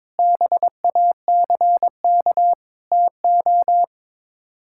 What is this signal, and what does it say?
Morse code, heard as a keyed tone, about 22 wpm, 700 Hz. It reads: BACK TO
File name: practice / 22wpm / Sets of 8 Words - 22wpm